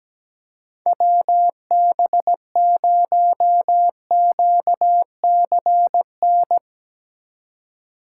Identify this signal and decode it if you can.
Morse code, heard as a keyed tone, about 17 wpm, 700 Hz. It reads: WB0QCN
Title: USA Call Signs - 6 Characters - 3x Character Spacing - 17wpm